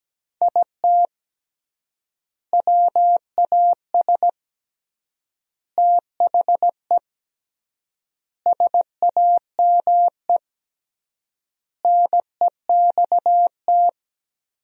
Morse code, heard as a keyed tone, about 17 wpm, 700 Hz. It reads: IT WAS THE SAME NEXT
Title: 3x Word-Spacing - Sentences from Top 500 Words - 17wpm